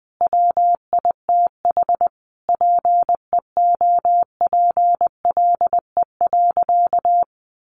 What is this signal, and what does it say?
Morse code, heard as a keyed tone, about 20 wpm, 700 Hz. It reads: WITH PEOPLE.
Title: Sentences from Top 1100 Words 20wpm